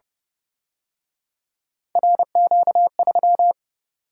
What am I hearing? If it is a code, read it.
Morse code, heard as a keyed tone, about 29 wpm, 700 Hz. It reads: RQ3